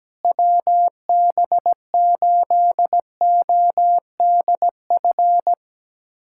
Morse code, heard as a keyed tone, about 17 wpm, 700 Hz. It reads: WB8ODF
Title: USA Callsign DE Callsign 17wpm